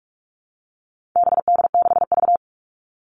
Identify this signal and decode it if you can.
Morse code, heard as a keyed tone, about 44 wpm, 700 Hz. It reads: BD6V
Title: All Cut Numbers - 4-Digit 45wpm